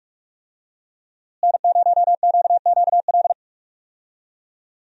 Morse code, heard as a keyed tone, about 46 wpm, 700 Hz. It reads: N0QQL
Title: USA Call Signs - 5 Characters - Repeated 3x - 45wpm